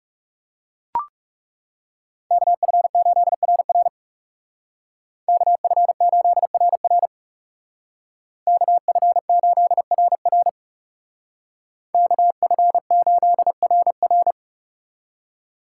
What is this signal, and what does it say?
Morse code, heard as a keyed tone, about 40 wpm, 700 Hz. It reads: KF8RR KF8RR KF8RR KF8RR